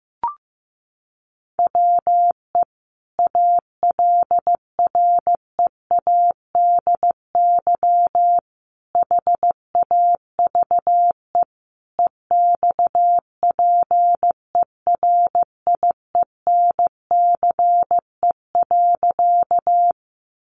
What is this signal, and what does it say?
Morse code, heard as a keyed tone, about 15 wpm, 700 Hz. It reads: WE ALREADY HAVE EXPERIENCE.
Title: Sentences from Top 700 Words 15wpm